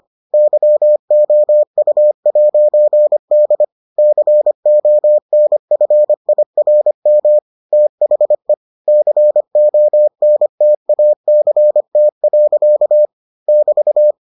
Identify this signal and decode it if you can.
Morse code, heard as a keyed tone, about 25 wpm, 600 Hz. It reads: YOU'D CONFIRM THE CONTACT. <BT>